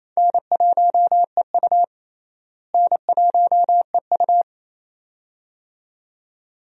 Morse code, heard as a keyed tone, about 28 wpm, 700 Hz. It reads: N1EU N1EU
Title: USA Call Signs - 4 Characters - Repeated 2x - 28wpm